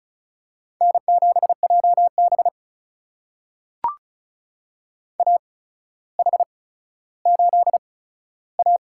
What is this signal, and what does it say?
Morse code, heard as a keyed tone, about 36 wpm, spaced as for 15 wpm, 700 Hz. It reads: N7JB AH8A
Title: USA Call Signs - 4 Characters - 8x Character Spacing - 35wpm